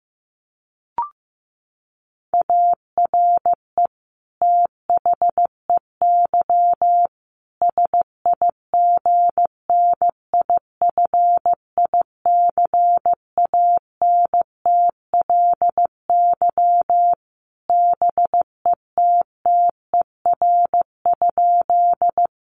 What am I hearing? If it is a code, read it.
Morse code, heard as a keyed tone, about 15 wpm, 700 Hz. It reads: ARE THEY SIGNIFICANTLY BETTER?
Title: Sentences from Top 1500 Words 15wpm